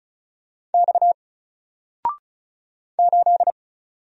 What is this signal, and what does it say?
Morse code, heard as a keyed tone, about 36 wpm, 700 Hz. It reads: X 8
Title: Single Letter-Number 35wpm